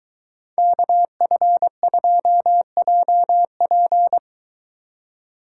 Morse code, heard as a keyed tone, about 22 wpm, 700 Hz. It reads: KF2JP